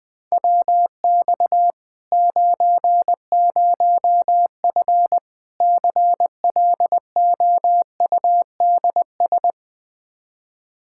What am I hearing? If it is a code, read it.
Morse code, heard as a keyed tone, about 21 wpm, 700 Hz. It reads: WX 90F CLOUDS
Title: Practice QSO - Exchange 2 - Info - 20wpm